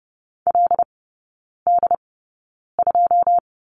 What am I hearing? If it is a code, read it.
Morse code, heard as a keyed tone, about 30 wpm, spaced as for 9 wpm, 700 Hz. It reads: LD2
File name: Lesson 18e - Letter m - Three Characters 30wpm